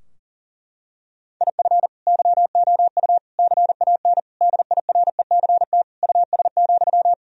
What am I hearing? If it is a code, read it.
Morse code, heard as a keyed tone, about 37 wpm, 700 Hz. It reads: IF YOU CAN DIRECT US,
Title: Sentences from Top 1000 Words - Review Part 2 - 40wpm